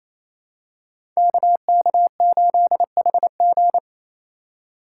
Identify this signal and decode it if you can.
Morse code, heard as a keyed tone, about 27 wpm, 700 Hz. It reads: KK8HG
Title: Code Groups - 5 Letter:Numbers - 28wpm